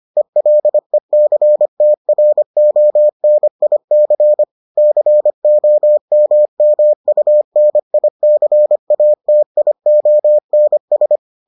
Morse code, heard as a keyed tone, about 25 wpm, 600 Hz. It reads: ELECTRONIC COMMUNICATIONS